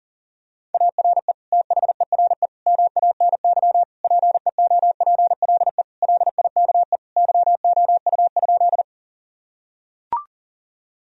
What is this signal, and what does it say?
Morse code, heard as a keyed tone, about 37 wpm, 700 Hz. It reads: ARE THERE MANY PEOPLE LIKE YOU?
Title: Sentences from Top 1000 Words - Review Part 1 - 40wpm